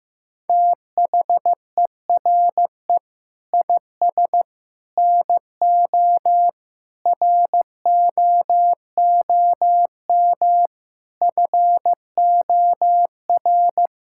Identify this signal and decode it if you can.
Morse code, heard as a keyed tone, about 15 wpm, 700 Hz. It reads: THERE IS NO ROOM FOR
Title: Sentences from Top 1000 Words - Review Part 5 - 15wpm